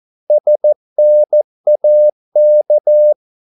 Morse code, heard as a keyed tone, about 14 wpm, 600 Hz. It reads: SNAK
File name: Korte ord DK 14wpm